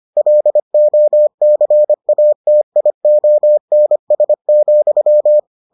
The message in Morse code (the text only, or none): LOCATIONS,